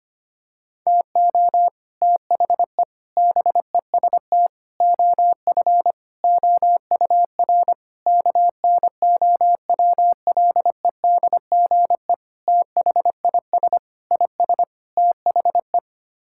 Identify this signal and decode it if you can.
Morse code, heard as a keyed tone, about 25 wpm, 700 Hz. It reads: TO THE BEST OF OUR KNOWLEDGE THIS IS THE
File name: Sets of 9 Words - 25wpm